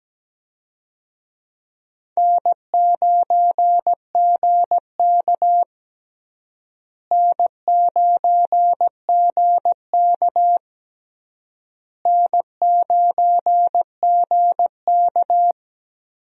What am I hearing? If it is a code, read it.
Morse code, heard as a keyed tone, about 17 wpm, 700 Hz. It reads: N9GK N9GK N9GK
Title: USA Call Signs - 4 Characters - Repeated 3x - 17wpm